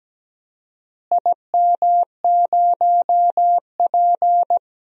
Morse code, heard as a keyed tone, about 17 wpm, 700 Hz. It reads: IM0P
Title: DXCC-100 Call Signs - 4 Characters - Repeated 3x - 17wpm